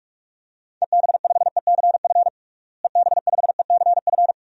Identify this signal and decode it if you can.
Morse code, heard as a keyed tone, about 46 wpm, 700 Hz. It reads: EB5ECF EB5ECF
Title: DXCC-100 Call Signs - 6 Characters - Repeated 2x - 45wpm